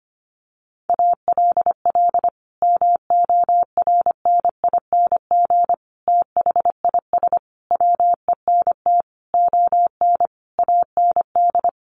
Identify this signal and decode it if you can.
Morse code, heard as a keyed tone, about 25 wpm, 700 Hz. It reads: ALL MORNING THIS WENT ON AND
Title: Sentences from Top 1500 Words 25wpm